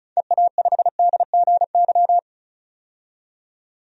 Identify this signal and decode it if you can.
Morse code, heard as a keyed tone, about 36 wpm, 700 Hz. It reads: EA5DGY
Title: DXCC-100 Call Signs - 6 Characters - 35wpm